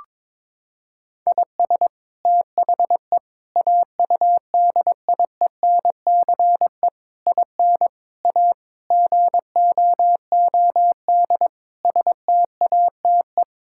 Morse code, heard as a keyed tone, about 22 wpm, 700 Hz. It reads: IS THE AUDIENCE IN A GOOD STATE
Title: Sentences from Top 1000 Words - Review Part 5 - 22wpm